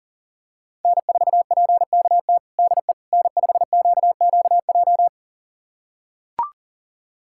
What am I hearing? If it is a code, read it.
Morse code, heard as a keyed tone, about 40 wpm, 700 Hz. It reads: N4PKT DE N5QQJ
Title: USA Callsign DE Callsign - 5 Characters - 40wpm